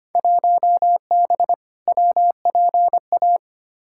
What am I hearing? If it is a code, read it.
Morse code, heard as a keyed tone, about 25 wpm, 700 Hz. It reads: 1B WPA
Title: Field Day Exchanges - 25wpm